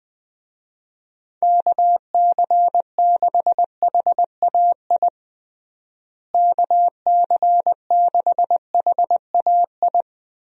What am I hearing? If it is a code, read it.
Morse code, heard as a keyed tone, about 20 wpm, 700 Hz. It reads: KC6HAI KC6HAI
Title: USA Call Signs - 6 Characters - Repeated 3x - 20wpm